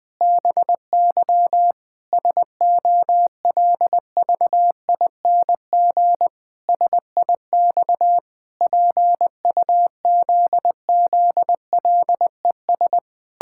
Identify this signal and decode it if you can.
Morse code, heard as a keyed tone, about 19 wpm, 700 Hz. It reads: BY SOLVING SIX PUZZLES
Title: Pangram Challenge v2 - 20wpm